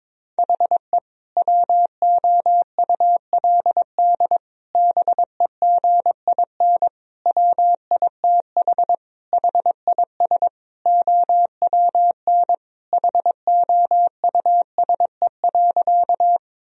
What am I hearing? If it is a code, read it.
Morse code, heard as a keyed tone, about 22 wpm, 700 Hz. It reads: HE WOULD BEGIN WITH HIS OWN HOUSE.
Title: Sentences from Top 1000 Words - Review Part 1 - 22wpm